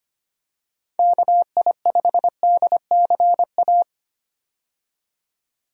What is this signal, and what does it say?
Morse code, heard as a keyed tone, about 25 wpm, 700 Hz. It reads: KI5DCA
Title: USA Call Signs - 6 Characters - 25wpm